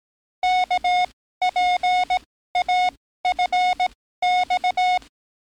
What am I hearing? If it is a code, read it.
Morse code, heard as a keyed tone, about 18 wpm, spaced as for 14 wpm, 730 Hz. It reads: KPAFX
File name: training / NAVPERS 11049to11060 / NAVPERS 11056 RA RB